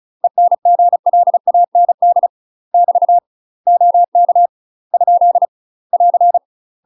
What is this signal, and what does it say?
Morse code, heard as a keyed tone, about 34 wpm, 700 Hz. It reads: ENGLAND <BT> OK ? <AR>
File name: UK MASTER QSOs TRAINING / 35WPM / UK MASTER QSOs 0040